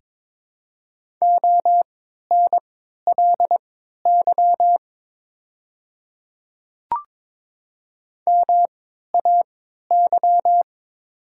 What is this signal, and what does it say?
Morse code, heard as a keyed tone, about 22 wpm, spaced as for 12 wpm, 700 Hz. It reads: ONLY MAY
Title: 3x Character Spacing - Top 200 Words - 22wpm